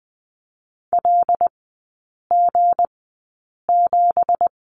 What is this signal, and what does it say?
Morse code, heard as a keyed tone, about 20 wpm, spaced as for 8 wpm, 700 Hz. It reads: LG7
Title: 166 Lesson 27e - Number 7 - Three Characters 20wpm